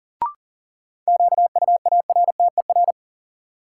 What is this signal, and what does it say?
Morse code, heard as a keyed tone, about 40 wpm, 700 Hz. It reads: QUARTER
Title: Sentences from Top 1300 Words - New Word Drill - 40wpm